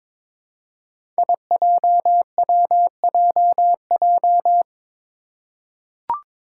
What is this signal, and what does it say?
Morse code, heard as a keyed tone, about 21 wpm, 700 Hz. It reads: IJWJJ
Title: Five Challenging Letter-Number Combinations 22wpm